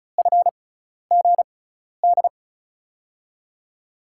Morse code, heard as a keyed tone, about 36 wpm, spaced as for 12 wpm, 700 Hz. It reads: FGD
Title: USA Call Signs - 6x Character Spacing - 35wpm